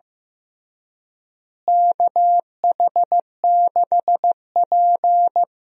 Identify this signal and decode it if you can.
Morse code, heard as a keyed tone, about 15 wpm, 700 Hz. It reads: KH6P
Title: DXCC-100 Call Signs - 4 Characters - Repeated 3x - 15wpm